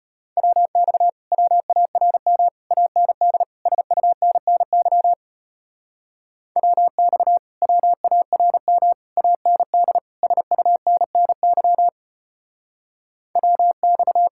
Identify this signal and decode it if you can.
Morse code, heard as a keyed tone, about 33 wpm, 700 Hz. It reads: WX WARM AND SUNNY WX WARM AND SUNNY WX